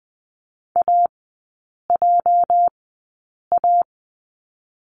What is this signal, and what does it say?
Morse code, heard as a keyed tone, about 20 wpm, spaced as for 8 wpm, 690 Hz. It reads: AJA
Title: 205 Lesson 33e - Letter j - Three Characters 20wpm